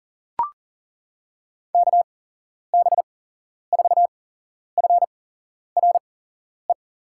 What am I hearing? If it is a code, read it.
Morse code, heard as a keyed tone, about 40 wpm, spaced as for 11 wpm, 700 Hz. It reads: KB4FRE
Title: USA Call Signs - 6 Characters - 8x Character Spacing - 40wpm